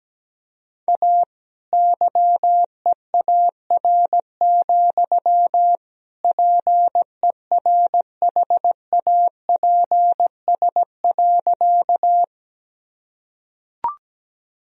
Morse code, heard as a keyed tone, about 17 wpm, 700 Hz. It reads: A YEAR, PERHAPS.
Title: Sentences from Top 1000 Words - Review Part 1 - 17wpm